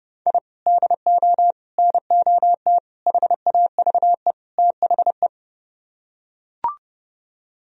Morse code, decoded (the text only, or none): I DO NOT HAVE THE